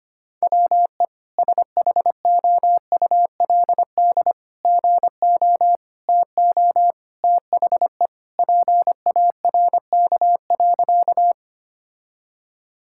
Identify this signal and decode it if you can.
Morse code, heard as a keyed tone, about 25 wpm, 700 Hz. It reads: WE SHOULD GO TO THE PARK.
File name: practice / 25wpm / Sentences with 3-Word Phrases - 25wpm